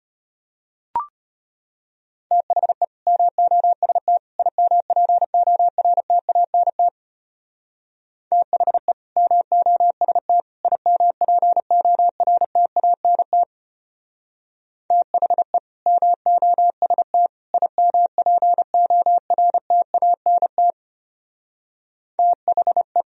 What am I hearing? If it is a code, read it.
Morse code, heard as a keyed tone, about 30 wpm, 700 Hz. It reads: THE MOST IMPORTANT THE MOST IMPORTANT THE MOST IMPORTANT THE
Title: Speed-Racing - Sets of 3 Words 25wpm